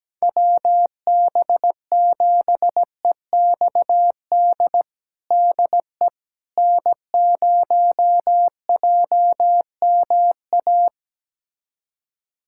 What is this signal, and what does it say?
Morse code, heard as a keyed tone, about 17 wpm, 700 Hz. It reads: WB7EXD DE N0JMA
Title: USA Callsign DE Callsign 17wpm